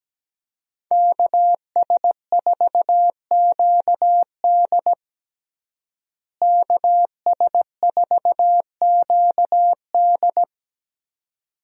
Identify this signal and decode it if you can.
Morse code, heard as a keyed tone, about 17 wpm, 700 Hz. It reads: KS4QD KS4QD
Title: USA Call Signs - 5 Characters - Repeated 2x - 17wpm